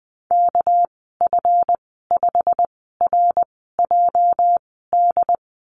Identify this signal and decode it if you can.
Morse code, heard as a keyed tone, about 20 wpm, spaced as for 15 wpm, 700 Hz. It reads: KF5RJD
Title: 219 Lesson 35g - Number 0 - Call Signs 20wpm